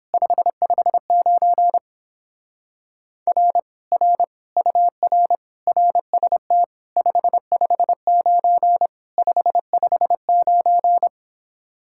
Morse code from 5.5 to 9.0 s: RST 559

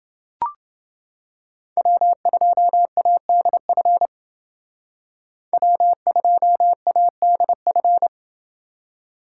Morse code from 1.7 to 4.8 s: W2ADF

